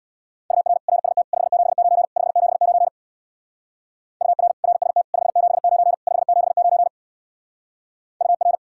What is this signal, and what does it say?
Morse code, heard as a keyed tone, about 60 wpm, 700 Hz. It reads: UR RST 579 579 UR RST 579 579 UR
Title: Speed-Racing Common QSO Phrases - 50wpm